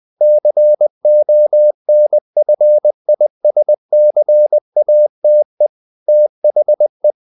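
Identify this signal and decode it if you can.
Morse code, heard as a keyed tone, about 20 wpm, 600 Hz. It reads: CONFISCATE THE
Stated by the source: Morse code at 19 wpm from CW med Gnister, reading Ken Follett's novel Never in English.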